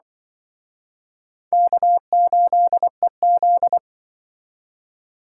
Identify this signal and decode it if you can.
Morse code, heard as a keyed tone, about 25 wpm, 700 Hz. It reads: K8EZ